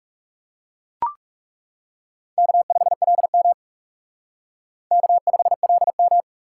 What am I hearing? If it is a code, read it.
Morse code, heard as a keyed tone, about 36 wpm, 700 Hz. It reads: K5LM K5LM
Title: Speed Racing - USA 1x2 Call Signs 30wpm